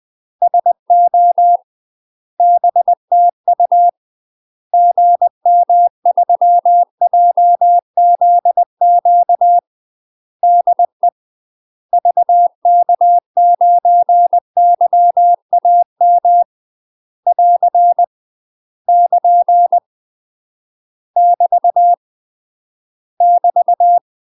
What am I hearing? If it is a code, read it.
Morse code, heard as a keyed tone, about 20 wpm, 700 Hz. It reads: SO BTU GM3JZQ DE VK9YAM <AR> <KN> <BT> <BT>
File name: UK MASTER QSOs TRAINING / 20WPM / UK MASTER QSOs 0039